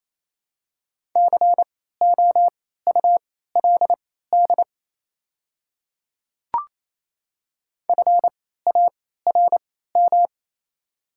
Morse code, heard as a keyed tone, about 27 wpm, spaced as for 16 wpm, 700 Hz. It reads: COULD FARM
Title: 3x Character Spacing - Top 500 Words - 28wpm